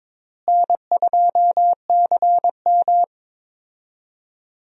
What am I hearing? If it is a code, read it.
Morse code, heard as a keyed tone, about 21 wpm, 700 Hz. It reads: N2CM
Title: USA Call Signs - 4 Characters - Repeated 3x - 22wpm